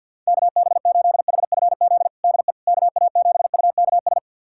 Morse code, heard as a keyed tone, about 52 wpm, 700 Hz. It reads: KB8HFZ DE KA7UKS
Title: USA Callsign DE Callsign - 6 Characters - 50wpm